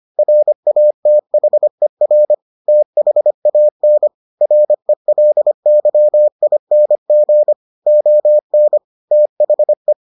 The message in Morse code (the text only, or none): RATHER THAN RELYING ON THE